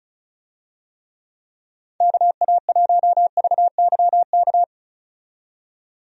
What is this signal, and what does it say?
Morse code, heard as a keyed tone, about 36 wpm, 700 Hz. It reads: KA1VYK
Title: USA Call Signs - 6 Characters - 35wpm